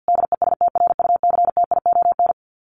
Morse code, heard as a keyed tone, about 48 wpm, 700 Hz. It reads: DESTRUCTION
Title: Sets of 5 Words Encore 50wpm